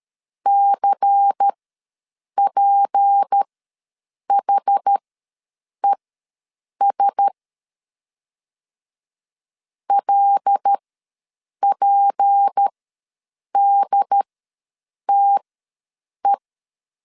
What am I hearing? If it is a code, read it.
Morse code, heard as a keyed tone, about 13 wpm, spaced as for 7 wpm, 800 Hz. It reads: CPHES LPDTE
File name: LEZIONI / 6 PKX0 / 6 P ws 12wpm 3117